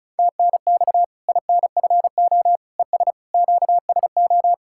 Morse code, heard as a keyed tone, about 36 wpm, 700 Hz. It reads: TNX INFO ES QSO
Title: Practice QSO - Exchange 4 - Ending - 35wpm